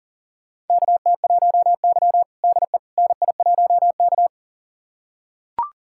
Morse code, heard as a keyed tone, about 37 wpm, 700 Hz. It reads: KT1Y DE NI1K
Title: USA Callsign DE Callsign - 4 Characters - 40wpm